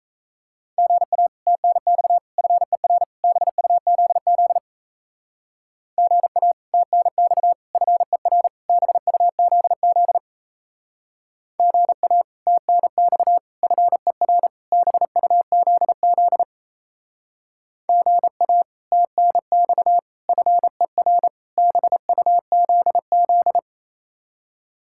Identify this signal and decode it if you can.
Morse code, heard as a keyed tone, about 36 wpm, 700 Hz. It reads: GA TNX FER BUZZ GA TNX FER BUZZ GA TNX FER BUZZ GA TNX FER BUZZ